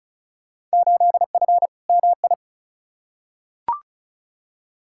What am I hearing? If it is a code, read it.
Morse code, heard as a keyed tone, about 36 wpm, 700 Hz. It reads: 8F MI
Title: Field Day Exchanges Encore - 35wpm